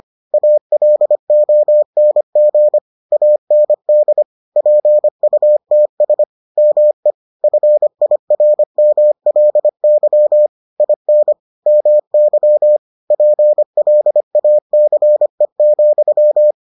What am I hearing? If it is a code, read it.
Morse code, heard as a keyed tone, about 25 wpm, 600 Hz. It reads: ALONG AND PUTS ME FIRMLY IN MY PLACE,